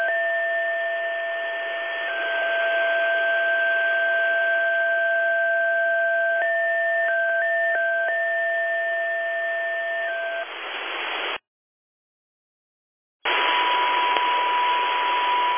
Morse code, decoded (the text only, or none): none